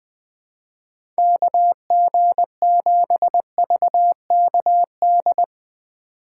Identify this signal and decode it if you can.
Morse code, heard as a keyed tone, about 20 wpm, 700 Hz. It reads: KG7VKD
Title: USA Call Signs - 3x Character Spacing - 20wpm